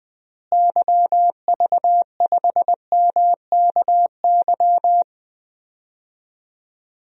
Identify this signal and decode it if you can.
Morse code, heard as a keyed tone, about 20 wpm, 700 Hz. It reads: YV5MKY
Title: DXCC-100 Call Signs - 6 Characters - Repeated 3x - 20wpm